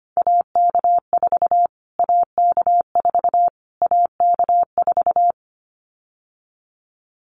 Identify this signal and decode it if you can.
Morse code, heard as a keyed tone, about 25 wpm, 700 Hz. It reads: AK4 AK4 AK4
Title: Callsign Prefixes for Top 25 DXCC - Repeated 3x - 25wpm